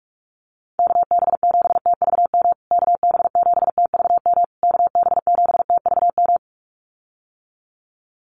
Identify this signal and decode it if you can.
Morse code, heard as a keyed tone, about 44 wpm, 690 Hz. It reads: KB7TVM KB7TVM KB7TVM
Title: USA Call Signs Repeated 3x 45wpm